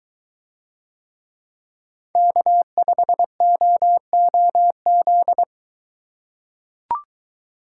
Morse code, heard as a keyed tone, about 21 wpm, 700 Hz. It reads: K5OOZ